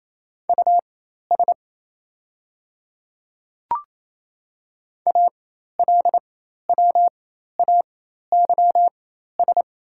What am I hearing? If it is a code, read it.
Morse code, heard as a keyed tone, about 28 wpm, spaced as for 13 wpm, 690 Hz. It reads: US ALWAYS
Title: 4x Character Spacing - Top 300 Words - 28wpm